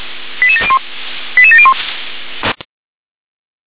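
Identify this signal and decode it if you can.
Five-tone selective call (ZVEI), heard as quick runs of five tones, tones of about 70 ms. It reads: 88701 88701
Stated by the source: ZVEI five-tone selective call received on an SDR.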